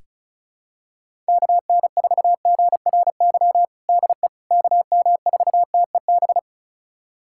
Morse code, heard as a keyed tone, about 36 wpm, 700 Hz. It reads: KN4GRY DE KM4TEB
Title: USA Callsign DE Callsign 35wpm